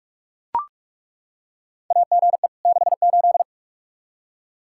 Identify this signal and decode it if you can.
Morse code, heard as a keyed tone, about 41 wpm, 700 Hz. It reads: AGE 68
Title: Practice QSO - Exchange 3 - More Info - 45wpm